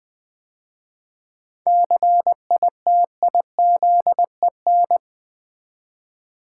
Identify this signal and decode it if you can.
Morse code, heard as a keyed tone, about 20 wpm, 690 Hz. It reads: CITIZEN
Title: 7 Letter Words - First 3 Letters Given - 20wpm